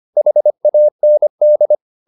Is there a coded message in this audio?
Yes: Morse code from 0.2 to 1.8 s, heard as a keyed tone, about 25 wpm, 600 Hz. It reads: HAND